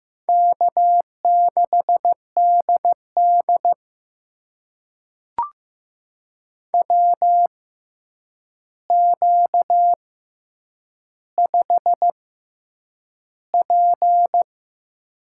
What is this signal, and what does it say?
Morse code, heard as a keyed tone, about 15 wpm, spaced as for 8 wpm, 700 Hz. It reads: K6DD WQ5P
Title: USA Call Signs - 4 Characters - 6x Character Spacing - 15wpm